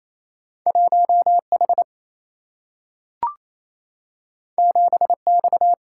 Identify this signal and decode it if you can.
Morse code, heard as a keyed tone, about 28 wpm, 700 Hz. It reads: 1 H 7 X
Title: Single-Letter Single-Number 28wpm